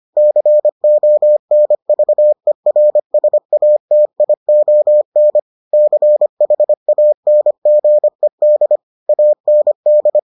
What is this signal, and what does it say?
Morse code, heard as a keyed tone, about 25 wpm, 600 Hz. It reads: CONVERSATION CHANGED AND